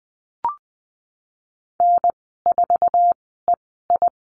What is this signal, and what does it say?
Morse code, heard as a keyed tone, about 19 wpm, spaced as for 14 wpm, 700 Hz. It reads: N4EI
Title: 048 Lesson 9g - Number 4 - Call Signs 20wpm